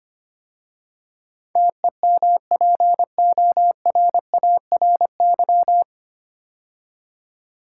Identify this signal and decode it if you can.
Morse code, heard as a keyed tone, about 25 wpm, 700 Hz. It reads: TEMPORARY
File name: practice / 25wpm / 9 Letter Words - First 3 Letters Given - 25wpm